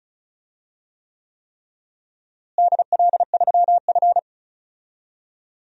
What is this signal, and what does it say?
Morse code, heard as a keyed tone, about 34 wpm, 700 Hz. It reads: DL3F
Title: DXCC-100 Call Signs - 4 Characters - Repeated 3x - 35wpm